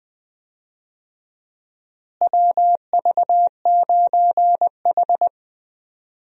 Morse code, heard as a keyed tone, about 21 wpm, 700 Hz. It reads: WV9H
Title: USA Call Signs - 4 Characters - Repeated 3x - 20wpm